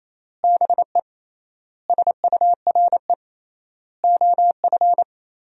Morse code, heard as a keyed tone, about 27 wpm, 700 Hz. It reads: BE SURE OF
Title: 3x Word-Spacing - Sentences from Top 300 Words - 28wpm